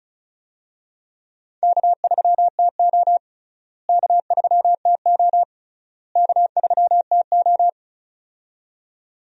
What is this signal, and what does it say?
Morse code, heard as a keyed tone, about 36 wpm, 700 Hz. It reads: K3TO K3TO K3TO
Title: USA Call Signs - 4 Characters - Repeated 3x - 35wpm